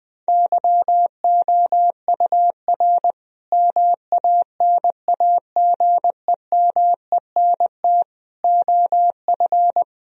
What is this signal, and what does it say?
Morse code, heard as a keyed tone, about 20 wpm, 700 Hz. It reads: YOUR MANAGEMENT OF
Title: Sentences from Top 1000 Words - Review Part 4 - 20wpm